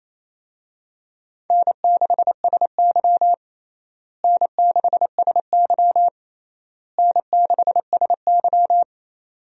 Morse code, heard as a keyed tone, about 28 wpm, 700 Hz. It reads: N6SY N6SY N6SY
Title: USA Call Signs - 4 Characters - Repeated 3x - 28wpm